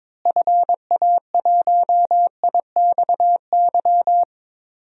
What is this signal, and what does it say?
Morse code, heard as a keyed tone, about 21 wpm, 700 Hz. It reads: FA1IXY
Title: DXCC-100 Call Signs - 6 Characters - Repeated 2x - 22wpm